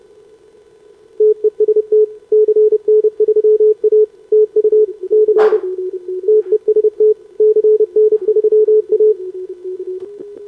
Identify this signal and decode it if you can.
Morse code, heard as a keyed tone, about 29 wpm, 440 Hz. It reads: TEST CN3A TU C TEST CN3A
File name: CQWWCW2025 / SobRanCN3A 10m